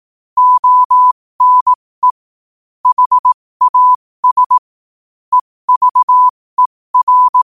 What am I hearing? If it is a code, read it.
Morse code, heard as a keyed tone, about 18 wpm, spaced as for 15 wpm, 1000 Hz. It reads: ONE HAS EVER
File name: qotdfiles / 15wpm